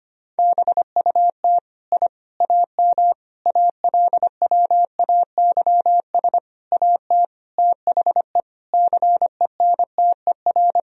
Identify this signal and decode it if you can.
Morse code, heard as a keyed tone, about 25 wpm, 700 Hz. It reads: BUT I AM ALWAYS AT THE CENTER
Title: Sentences from Top 1000 Words - Review Part 2 - 25wpm